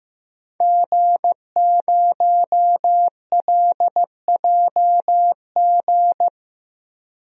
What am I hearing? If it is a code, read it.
Morse code, heard as a keyed tone, about 15 wpm, 700 Hz. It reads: G0LJG
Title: DXCC-100 Call Signs - 5 Characters - 15wpm